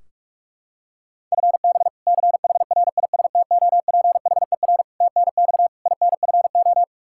Morse code, heard as a keyed tone, about 46 wpm, 700 Hz. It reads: FB CHRISTOPHER TNX INFO
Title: K1USN SST QSO Practice 45wpm